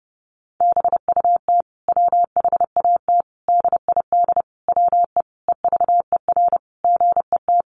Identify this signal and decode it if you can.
Morse code, heard as a keyed tone, about 30 wpm, 700 Hz. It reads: BUT WHAT DID WE EVER GET
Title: Sentences from Top 400 Words 30wpm